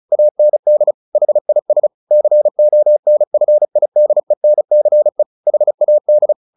Morse code, heard as a keyed tone, about 34 wpm, 600 Hz. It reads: AND HIS CONFIDENCE HAD
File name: the old man and the sea / the old man and the sea 1 35wpm